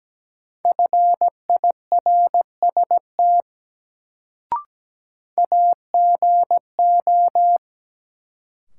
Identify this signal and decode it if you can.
Morse code, heard as a keyed tone, about 17 wpm, 700 Hz. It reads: FIRST AGO
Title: Sets of 2 Words - New Word Drill - 17wpm